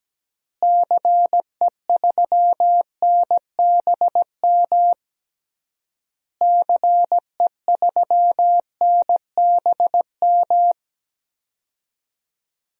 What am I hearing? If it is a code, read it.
Morse code, heard as a keyed tone, about 17 wpm, 700 Hz. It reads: CE3NBM CE3NBM
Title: DXCC-100 Call Signs - 6 Characters - Repeated 3x - 17wpm